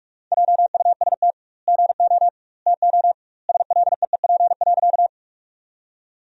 Morse code, heard as a keyed tone, about 46 wpm, 700 Hz. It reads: JUST GO TO SLEEP.
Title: 2x Word-Spacing - Sentences from Top 500 Words - 45wpm